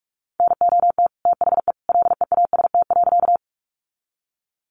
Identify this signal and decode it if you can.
Morse code, heard as a keyed tone, about 44 wpm, 700 Hz. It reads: NOT THE LEAST.
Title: Sentences from Top 700 Words 45wpm